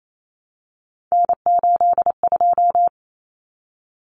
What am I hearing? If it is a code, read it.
Morse code, heard as a keyed tone, about 28 wpm, 700 Hz. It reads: N82
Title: Common Cut Numbers - 3-Digit 28wpm